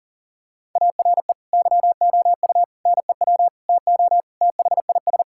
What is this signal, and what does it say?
Morse code, heard as a keyed tone, about 40 wpm, 700 Hz. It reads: ARE YOU NEW TO THIS
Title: Sentences from Top 1000 Words - Review Part 6 - 40wpm